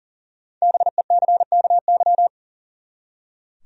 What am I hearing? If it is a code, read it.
Morse code, heard as a keyed tone, about 40 wpm, 700 Hz. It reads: BECKY
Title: USA Common Names - Max 5 Letters - 40wpm